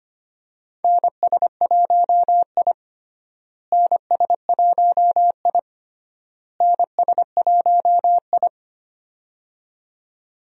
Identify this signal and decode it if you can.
Morse code, heard as a keyed tone, about 25 wpm, 700 Hz. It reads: NS1I NS1I NS1I
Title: USA Call Signs - 4 Characters - Repeated 3x - 25wpm